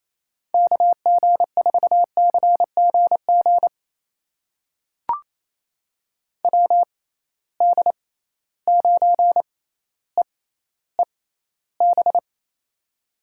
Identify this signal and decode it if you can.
Morse code, heard as a keyed tone, about 28 wpm, spaced as for 14 wpm, 700 Hz. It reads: KG4CGG WD9EEB
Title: USA Call Signs - 6 Characters - 6x Character Spacing - 28wpm